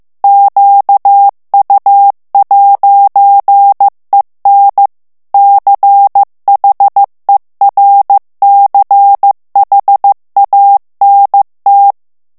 Morse code, heard as a keyed tone, about 15 wpm, 800 Hz. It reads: QU'EN CHERCHANT
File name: FRANCAIS TEXTE CLAIR 2 VITESSE 14 MOTS MINUTES